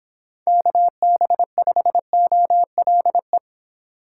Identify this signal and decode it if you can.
Morse code, heard as a keyed tone, about 26 wpm, 700 Hz. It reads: KB5OLE